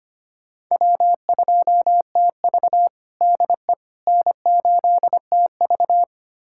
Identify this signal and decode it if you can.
Morse code, heard as a keyed tone, about 25 wpm, 700 Hz. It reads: W2TV DE N8TV
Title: USA Callsign DE Callsign - 4 Characters - 25wpm